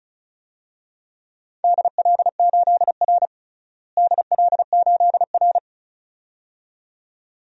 Morse code, heard as a keyed tone, about 36 wpm, 700 Hz. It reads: DL8R DL8R
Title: DXCC-100 Call Signs - 4 Characters - Repeated 2x - 35wpm